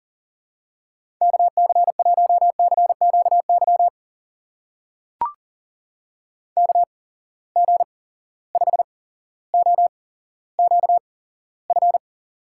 Morse code, heard as a keyed tone, about 40 wpm, spaced as for 17 wpm, 700 Hz. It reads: KC1CQY KG5OQF
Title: USA Call Signs - 6 Characters - 8x Character Spacing - 40wpm